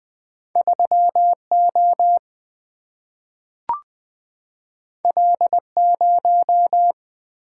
Morse code, heard as a keyed tone, about 20 wpm, 700 Hz. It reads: 3 O L 0
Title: Single-Letter Single-Number 20wpm